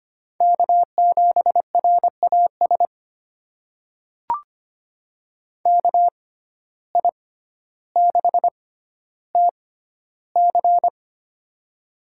Morse code, heard as a keyed tone, about 25 wpm, spaced as for 13 wpm, 700 Hz. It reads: K7RAS KI6TC
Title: USA Call Signs - 5 Characters - 6x Character Spacing - 25wpm